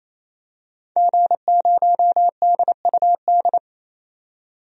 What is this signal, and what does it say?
Morse code, heard as a keyed tone, about 28 wpm, 700 Hz. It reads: G0DUD
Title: DXCC-100 Call Signs - 5 Characters - 28wpm